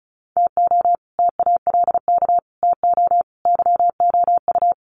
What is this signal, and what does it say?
Morse code, heard as a keyed tone, about 34 wpm, 700 Hz. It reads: TO TALK TO YOU
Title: Sets of 4 Words 35wpm